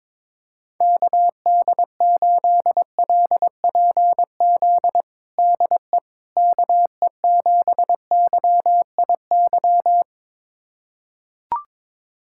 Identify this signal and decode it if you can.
Morse code, heard as a keyed tone, about 21 wpm, 700 Hz. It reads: KD8LPZ DE KE7YIY
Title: USA Callsign DE Callsign - 6 Characters - 22wpm